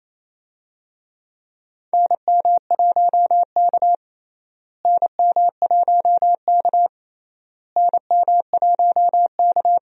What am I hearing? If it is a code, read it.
Morse code, heard as a keyed tone, about 28 wpm, 700 Hz. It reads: NM1K NM1K NM1K
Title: USA Call Signs - 4 Characters - Repeated 3x - 28wpm